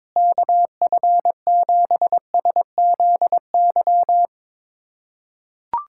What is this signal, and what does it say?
Morse code, heard as a keyed tone, about 22 wpm, 700 Hz. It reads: KF7SZY
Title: USA Callsign DE Callsign - 6 Characters - 22wpm